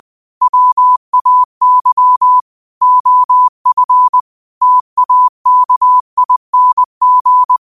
Morse code, heard as a keyed tone, about 20 wpm, 1000 Hz. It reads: WAY OF TAKING
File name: qotdfiles / 20wpm